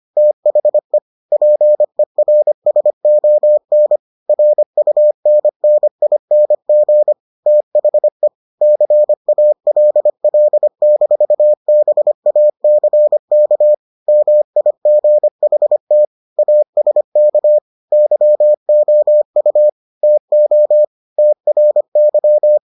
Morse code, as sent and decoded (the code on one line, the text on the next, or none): - .... / .--. .-. ... --- -. / .-. ..- -. -. .. -. --. / - .... / -.-. .- .-.. .-.. -....- -... .- -.-. -.- / -- .. --. .... - / .- ... -.- / -.-- --- ..- / - --- / - .-. -.--
THE PERSON RUNNING THE CALL-BACK MIGHT ASK YOU TO TRY